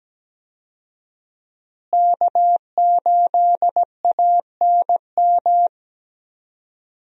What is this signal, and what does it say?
Morse code, heard as a keyed tone, about 17 wpm, 700 Hz. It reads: K8ANM
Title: USA Call Signs - 5 Characters - Repeated 3x - 17wpm